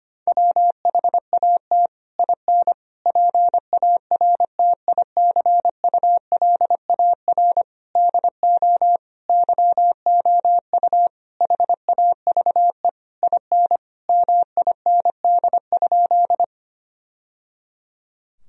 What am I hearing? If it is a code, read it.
Morse code, heard as a keyed tone, about 25 wpm, 700 Hz. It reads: WHAT IN PARTICULAR DO YOU HAVE IN MIND?
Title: Sentences from Top 1000 Words - Review Part 4 - 25wpm